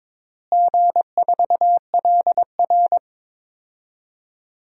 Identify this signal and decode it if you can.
Morse code, heard as a keyed tone, about 22 wpm, 700 Hz. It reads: G4LR
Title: DXCC-100 Call Signs - 4 Characters - Repeated 2x - 22wpm